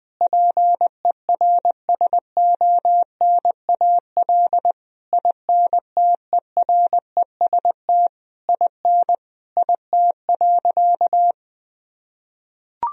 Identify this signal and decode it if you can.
Morse code, heard as a keyed tone, about 20 wpm, 700 Hz. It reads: PERSONAL INTEREST IN IT.
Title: Sentences from Top 1000 Words - Review Part 4 - 20wpm